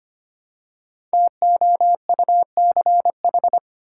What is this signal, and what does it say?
Morse code, heard as a keyed tone, about 25 wpm, 700 Hz. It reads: TOUCH
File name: 5 Letter Words - First 3 Letters Given - 25wpm